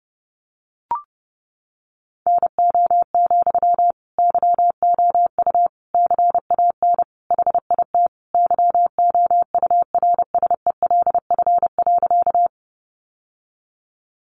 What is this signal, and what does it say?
Morse code, heard as a keyed tone, about 30 wpm, 700 Hz. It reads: NO, YOU CAN HIT YOURSELF.
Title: Sentences from Top 1200 Words 30wpm